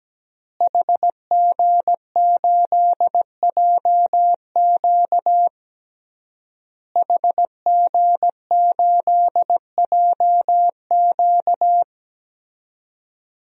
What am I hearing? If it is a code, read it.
Morse code, heard as a keyed tone, about 17 wpm, 700 Hz. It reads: HG8JQ HG8JQ
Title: DXCC-100 Call Signs - 5 Characters - Repeated 3x - 17wpm